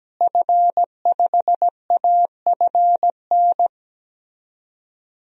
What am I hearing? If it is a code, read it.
Morse code, heard as a keyed tone, about 17 wpm, 700 Hz. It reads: F5AFN
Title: DXCC-100 Call Signs - 5 Characters - Repeated 3x - 17wpm